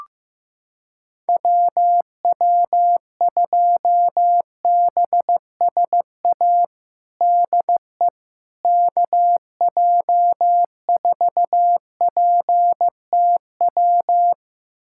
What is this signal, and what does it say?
Morse code, heard as a keyed tone, about 16 wpm, 700 Hz. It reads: WW2BSA DE KJ4PTW
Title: USA Callsign DE Callsign - 6 Characters - 15wpm